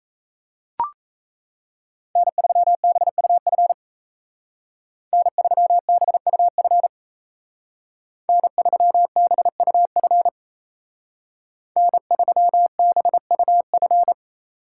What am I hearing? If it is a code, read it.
Morse code, heard as a keyed tone, about 34 wpm, 700 Hz. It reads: N3BUF N3BUF N3BUF N3BUF
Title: Speed Racing - USA Call Signs 28wpm